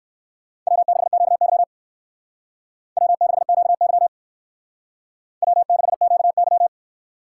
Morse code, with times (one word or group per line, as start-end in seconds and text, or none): 0.7-1.6 W6QY
3.0-4.1 W6QY
5.4-6.7 W6QY